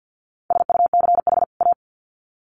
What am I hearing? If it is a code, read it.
Morse code, heard as a keyed tone, about 50 wpm, 700 Hz. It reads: SUCH A
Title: Sets of 2 Words 50wpm